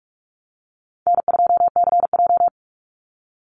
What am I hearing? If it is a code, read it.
Morse code, heard as a keyed tone, about 44 wpm, 700 Hz. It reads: N2CJ
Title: Easy Call Signs Repeated 3x - 45wpm